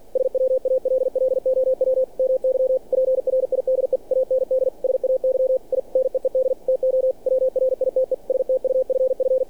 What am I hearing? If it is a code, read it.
Morse code, heard as a keyed tone, about 48 wpm, 520 Hz. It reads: SWALLOW MY PRIDE AND SAY I NEED TO WRITE STUFF
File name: archived / CRSnet